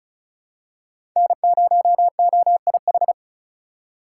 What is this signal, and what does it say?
Morse code, heard as a keyed tone, about 36 wpm, 700 Hz. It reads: N0OIH
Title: USA Call Signs - 5 Characters - 4x Character Spacing - 35wpm